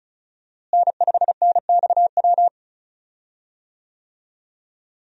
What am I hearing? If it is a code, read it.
Morse code, heard as a keyed tone, about 36 wpm, 700 Hz. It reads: N5NXW
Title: USA Call Signs - 5 Characters - Repeated 2x - 35wpm